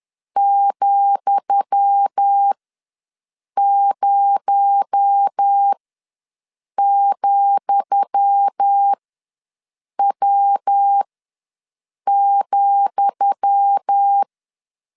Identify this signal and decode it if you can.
Morse code, heard as a keyed tone, about 10 wpm, spaced as for 6 wpm, 800 Hz. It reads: ,0,W,
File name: LEZIONI / 10 SIGNS / 10D COMA ws 10wpm 4314